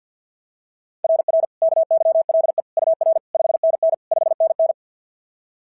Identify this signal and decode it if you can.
Morse code, heard as a keyed tone, about 50 wpm, 690 Hz. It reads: RR KYLE UR 5NN 5NN
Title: SOTA QSO Practice 50wpm